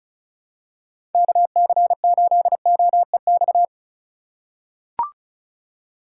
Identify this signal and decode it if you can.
Morse code, heard as a keyed tone, about 34 wpm, 700 Hz. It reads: KC8OEX
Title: USA Call Signs - 6 Characters - 2x Character Spacing - 35wpm